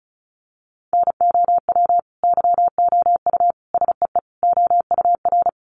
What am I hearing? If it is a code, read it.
Morse code, heard as a keyed tone, about 34 wpm, 700 Hz. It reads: NOW YOU SEE OUR
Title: Sentences from Top 1500 Words 35wpm